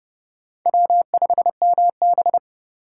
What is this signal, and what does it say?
Morse code, heard as a keyed tone, about 27 wpm, 700 Hz. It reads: W5MB